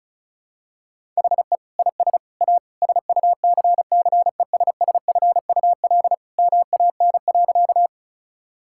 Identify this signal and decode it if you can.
Morse code, heard as a keyed tone, about 34 wpm, 700 Hz. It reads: HE IS A SUCCESSFUL MAN.
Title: Sentences from Top 1000 Words - Review Part 6 - 35wpm